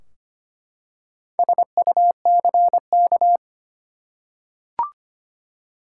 Morse code, heard as a keyed tone, about 25 wpm, 700 Hz. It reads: SUCK
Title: Four Letter Words - 25wpm